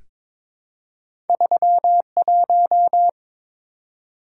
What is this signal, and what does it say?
Morse code, heard as a keyed tone, about 22 wpm, 690 Hz. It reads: 31
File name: practice / 22wpm / Two Letter-Number Combinations - 22wpm